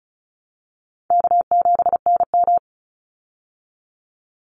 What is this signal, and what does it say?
Morse code, heard as a keyed tone, about 34 wpm, 700 Hz. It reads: K7NM
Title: Call Signs Repeated 3x - 35wpm